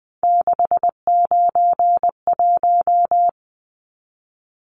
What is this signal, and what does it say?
Morse code, heard as a keyed tone, about 20 wpm, 700 Hz. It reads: 691
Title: Numbers - 3-Digit 20wpm